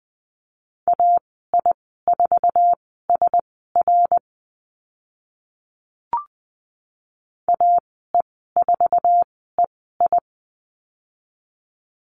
Morse code, heard as a keyed tone, about 20 wpm, spaced as for 15 wpm, 700 Hz. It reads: AI4SR AE4EI
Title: 082 Lesson 14g - Number 2 - Call Signs 20wpm